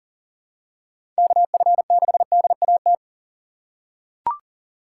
Morse code, heard as a keyed tone, about 40 wpm, 700 Hz. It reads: KF6DAT
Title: USA Call Signs - 3x Character Spacing - 40wpm